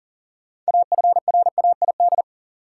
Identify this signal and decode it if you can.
Morse code, heard as a keyed tone, about 38 wpm, 700 Hz. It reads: AFRAID
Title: Six Letter Words - 40wpm